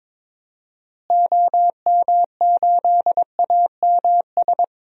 Morse code, heard as a keyed tone, about 22 wpm, 700 Hz. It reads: OM8AMS
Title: DXCC-100 Call Signs - 6 Characters - Repeated 3x - 22wpm